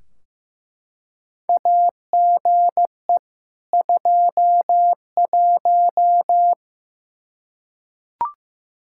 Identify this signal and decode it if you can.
Morse code, heard as a keyed tone, about 16 wpm, 700 Hz. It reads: AGE 21
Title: Practice QSO - Exchange 3 - More Info - 15wpm